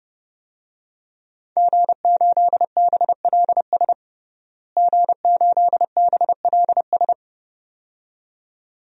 Morse code, heard as a keyed tone, about 30 wpm, 700 Hz. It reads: G8BLS G8BLS
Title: DXCC-100 Call Signs - 5 Characters - Repeated 2x - 30wpm